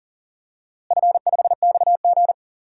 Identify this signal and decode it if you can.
Morse code, heard as a keyed tone, about 39 wpm, 700 Hz. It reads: F5XG
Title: DXCC-100 Call Signs - 4 Characters - Repeated 3x - 40wpm